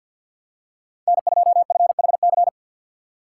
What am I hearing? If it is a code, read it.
Morse code, heard as a keyed tone, about 53 wpm, 700 Hz. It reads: N2FHC
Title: USA Call Signs - 5 Characters - 8x Character Spacing - 50wpm